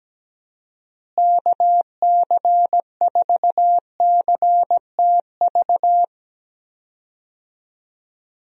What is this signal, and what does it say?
Morse code, heard as a keyed tone, about 17 wpm, 700 Hz. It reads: KC4CTV
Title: USA Call Signs - 6 Characters - Repeated 2x - 17wpm